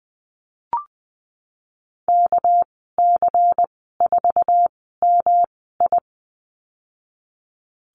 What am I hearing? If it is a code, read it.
Morse code, heard as a keyed tone, about 20 wpm, spaced as for 15 wpm, 700 Hz. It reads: KC4MI
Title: 200 Lesson 32g - Letter k - Call Signs 20wpm